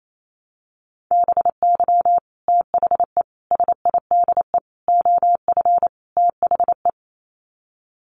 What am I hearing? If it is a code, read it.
Morse code, heard as a keyed tone, about 28 wpm, 700 Hz. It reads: BY THE SIDE OF THE
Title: Sets of 5 Words Encore2 28wpm